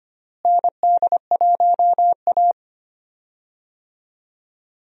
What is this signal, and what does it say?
Morse code, heard as a keyed tone, about 25 wpm, 700 Hz. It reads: ND1A
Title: USA Call Signs - 4 Characters - Repeated 3x - 25wpm